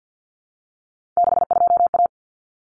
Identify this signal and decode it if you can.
Morse code, heard as a keyed tone, about 50 wpm, 700 Hz. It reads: 62A
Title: Common Cut Numbers - 3-Digit 50wpm